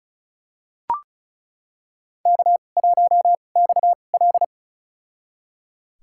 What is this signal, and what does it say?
Morse code, heard as a keyed tone, about 36 wpm, spaced as for 25 wpm, 700 Hz. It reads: K1XL
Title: USA Call Signs - 4 Characters - 2x Character Spacing - 35wpm